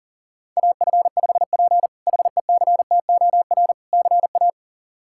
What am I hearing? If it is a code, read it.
Morse code, heard as a keyed tone, about 40 wpm, 700 Hz. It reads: AF5P HECTOR CA
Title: K1USN SST QSO Practice - 40wpm